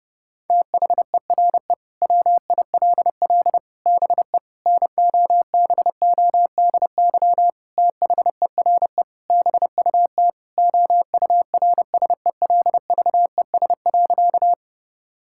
Morse code, heard as a keyed tone, about 32 wpm, 700 Hz. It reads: THERE WILL BE NOBODY THERE BUT OURSELVES.
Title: Sentences from Top 1600 Words 30wpm